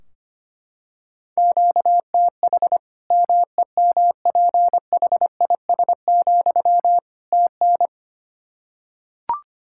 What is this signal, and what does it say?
Morse code, heard as a keyed tone, about 25 wpm, 700 Hz. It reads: QTH MEMPHIS, TN
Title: Practice QSO - Exchange 1 - Beginning - 25wpm